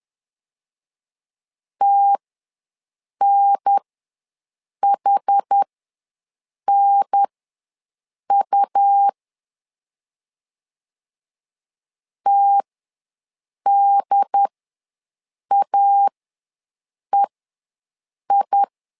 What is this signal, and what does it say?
Morse code, heard as a keyed tone, about 10 wpm, spaced as for 6 wpm, 800 Hz. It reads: TNHNU TDAEI
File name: LEZIONI / 2 UHDN / 2 UHDN ws 10wpm 2044